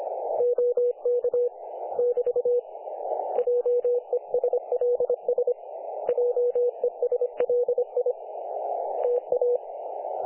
Morse code, heard as a keyed tone, about 37 wpm, spaced as for 32 wpm, 500 Hz, with radio noise. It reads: OK <BT> JESLS JESLI TA
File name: dlazaawansowanych / nagraniazpasma / websdr recording 3564.7kHz